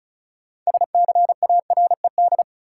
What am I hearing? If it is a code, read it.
Morse code, heard as a keyed tone, about 36 wpm, 700 Hz. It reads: SCARED
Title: Sentences from Top 700 Words - New Word Drill - 35wpm